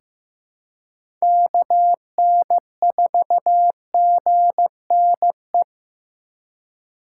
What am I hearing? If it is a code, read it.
Morse code, heard as a keyed tone, about 15 wpm, 700 Hz. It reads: KN4GNE
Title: USA Call Signs - 6 Characters - 2x Character Spacing - 15wpm